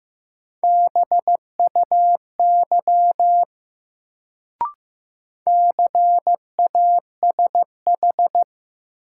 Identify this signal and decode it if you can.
Morse code, heard as a keyed tone, about 15 wpm, 700 Hz. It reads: BUY CASH
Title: Sentences from Top 700 Words - New Word Drill - 15wpm